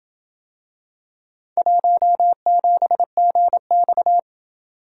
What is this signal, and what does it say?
Morse code, heard as a keyed tone, about 28 wpm, 700 Hz. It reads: 17GX